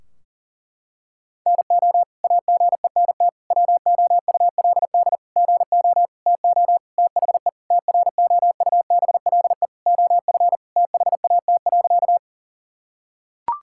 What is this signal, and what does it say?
Morse code, heard as a keyed tone, about 40 wpm, 700 Hz. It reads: NO AGENT WOULD GO TO THE TROUBLE OF THAT.
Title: Sentences from Top 1600 Words 40wpm